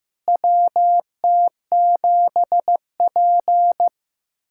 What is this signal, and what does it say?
Morse code, heard as a keyed tone, about 15 wpm, 700 Hz. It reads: WT7P
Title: USA Callsign DE Callsign - 4 Characters - 15wpm